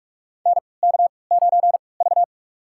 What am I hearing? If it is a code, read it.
Morse code, heard as a keyed tone, about 44 wpm, spaced as for 25 wpm, 700 Hz. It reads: NK9V
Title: USA Call Signs - 3x Character Spacing - 45wpm